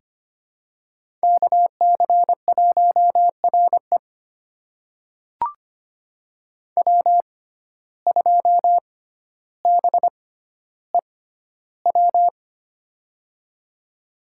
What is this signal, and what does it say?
Morse code, heard as a keyed tone, about 25 wpm, spaced as for 13 wpm, 700 Hz. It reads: KC1RE W2BEW
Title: USA Call Signs - 5 Characters - 6x Character Spacing - 25wpm